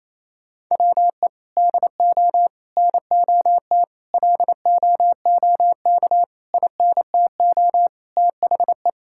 Morse code, heard as a keyed tone, about 28 wpm, 700 Hz. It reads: WE DO NOT LOOK INTO THE
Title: Sentences from Top 1600 Words 28wpm